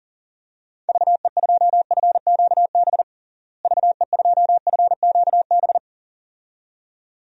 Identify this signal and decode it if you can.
Morse code, heard as a keyed tone, about 40 wpm, 700 Hz. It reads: VE2FQB VE2FQB
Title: DXCC-100 Call Signs - 6 Characters - Repeated 2x - 40wpm